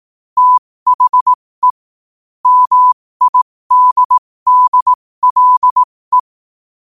Morse code, heard as a keyed tone, about 18 wpm, spaced as for 15 wpm, 1000 Hz. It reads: THE MIDDLE
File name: qotdfiles / 15wpm